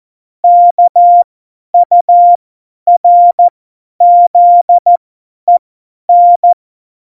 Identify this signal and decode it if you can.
Morse code, heard as a keyed tone, about 14 wpm, spaced as for 10 wpm, 700 Hz. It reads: KURZEN
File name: Morsekurs / Witze / Witz 02. Januar 2018, 50 F70